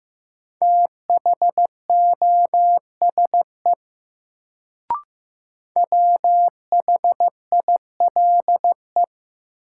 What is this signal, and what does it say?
Morse code, heard as a keyed tone, about 15 wpm, 700 Hz. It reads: THOSE WHILE
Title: Sentences from Top 300 Words - New Word Drill - 15wpm